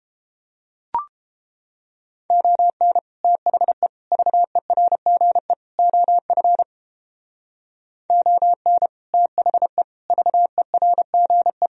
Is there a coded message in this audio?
Yes: Morse code from 2.3 to 11.7 s, heard as a keyed tone, about 31 wpm, 700 Hz. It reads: ON THE VERGE OF ON THE VERGE